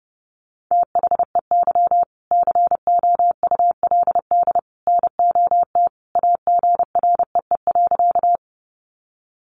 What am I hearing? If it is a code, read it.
Morse code, heard as a keyed tone, about 30 wpm, 700 Hz. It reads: THEY COULD NOT AGREE.
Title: Sentences from Top 1100 Words 30wpm